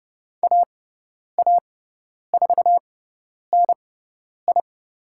Morse code, heard as a keyed tone, about 30 wpm, spaced as for 10 wpm, 700 Hz. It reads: AA4NI
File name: Lesson 17g - Letter u - Call Signs 30wpm